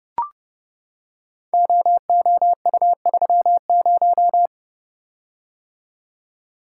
Morse code, heard as a keyed tone, about 30 wpm, 700 Hz. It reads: OOU30
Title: Code Groups - 5 Letter:Numbers - 30wpm